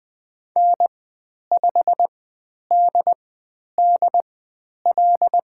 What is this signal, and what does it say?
Morse code, heard as a keyed tone, about 20 wpm, spaced as for 10 wpm, 700 Hz. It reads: N5DDL
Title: 102 Lesson 17g - Letter u - Call Signs 20wpm